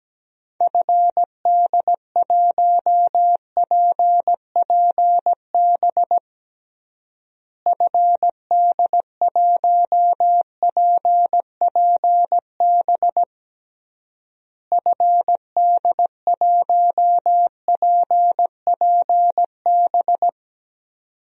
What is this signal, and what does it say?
Morse code, heard as a keyed tone, about 17 wpm, 700 Hz. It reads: FD1PPB FD1PPB FD1PPB
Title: DXCC-100 Call Signs - 6 Characters - Repeated 3x - 17wpm